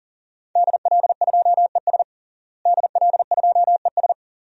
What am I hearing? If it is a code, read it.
Morse code, heard as a keyed tone, about 40 wpm, 700 Hz. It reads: DL2ES DL2ES
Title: DXCC-100 Call Signs - 5 Characters - Repeated 2x - 40wpm